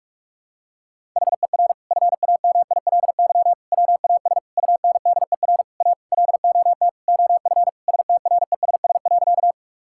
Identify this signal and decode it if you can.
Morse code, heard as a keyed tone, about 46 wpm, 700 Hz. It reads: HER FAMILY WAS UNDER A LOT OF STRESS.
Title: Sentences from Top 1000 Words - Review Part 6 - 45wpm